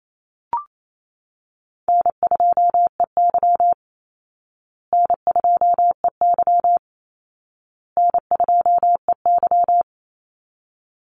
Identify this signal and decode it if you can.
Morse code, heard as a keyed tone, about 28 wpm, 700 Hz. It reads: N2EY N2EY N2EY
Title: Easy Call Signs Repeated 3x - 28wpm